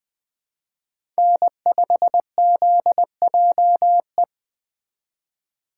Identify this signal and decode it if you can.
Morse code, heard as a keyed tone, about 24 wpm, 700 Hz. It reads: N5ZJE
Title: Speed Racing - USA 1x3 Call Signs 20wpm